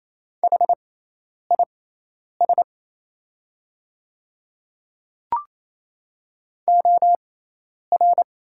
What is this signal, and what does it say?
Morse code, heard as a keyed tone, about 28 wpm, spaced as for 10 wpm, 700 Hz. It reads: HIS OR
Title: 6x Character Spacing - Top 100 Words - 28wpm